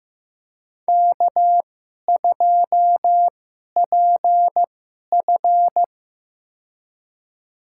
Morse code, heard as a keyed tone, about 15 wpm, spaced as for 11 wpm, 700 Hz. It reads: K2PF
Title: USA Call Signs - 2x Character Spacing - 15wpm